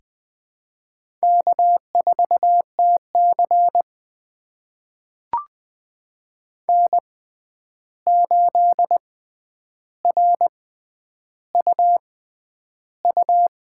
Morse code, heard as a keyed tone, about 20 wpm, spaced as for 10 wpm, 700 Hz. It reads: K4TC N8RUU
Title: USA Call Signs - 6x Character Spacing - 20wpm